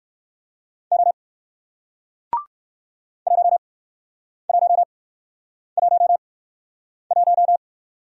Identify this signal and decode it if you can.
Morse code, heard as a keyed tone, about 59 wpm, spaced as for 9 wpm, 700 Hz. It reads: O 1111